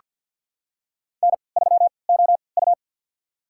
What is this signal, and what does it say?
Morse code, heard as a keyed tone, about 49 wpm, spaced as for 28 wpm, 700 Hz. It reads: N3OU
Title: USA Call Signs - 4 Characters - 3x Character Spacing - 50wpm